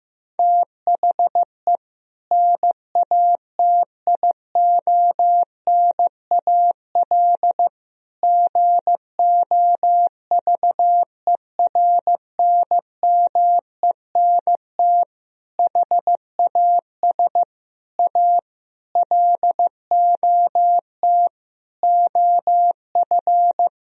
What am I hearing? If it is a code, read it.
Morse code, heard as a keyed tone, about 15 wpm, 700 Hz. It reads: THE NATIONAL GOVERNMENT HAS A LOT OF
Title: Sentences from Top 1000 Words - Review Part 4 - 15wpm